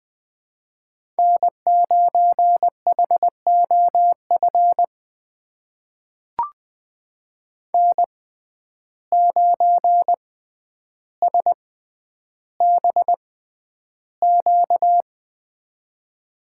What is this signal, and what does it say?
Morse code, heard as a keyed tone, about 20 wpm, spaced as for 10 wpm, 700 Hz. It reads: N9HOF N9SBQ
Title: USA Call Signs - 5 Characters - 6x Character Spacing - 20wpm